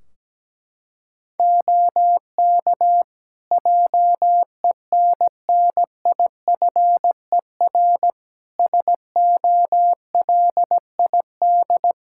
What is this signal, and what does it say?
Morse code, heard as a keyed tone, about 17 wpm, 700 Hz. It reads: OK JENNIFER SOLID
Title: Practice QSO - Exchange 3 - More Info - 17wpm